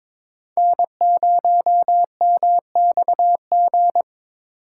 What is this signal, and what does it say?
Morse code, heard as a keyed tone, about 23 wpm, 700 Hz. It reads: N0MXG
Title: USA Call Signs - 4x Character Spacing - 22wpm